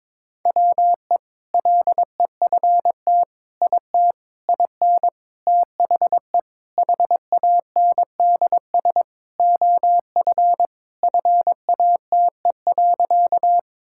Morse code, heard as a keyed tone, about 22 wpm, 700 Hz. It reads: WE LEFT IT IN THE HANDS OF FATE.
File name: practice / 22wpm / Sentences with 4-Word Phrases - 22wpm